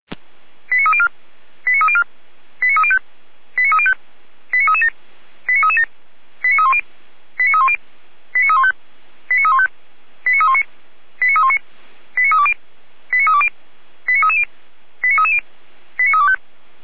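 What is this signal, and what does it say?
Five-tone selective call (ZVEI), heard as quick runs of five tones, tones of about 70 ms. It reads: 89304 89304 89305 89305 89337 89337 89310 89310 89315 89315 89319 89319 89320 89320 89330 89330 89325
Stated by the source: Five-tone ZVEI selective call received by SDR.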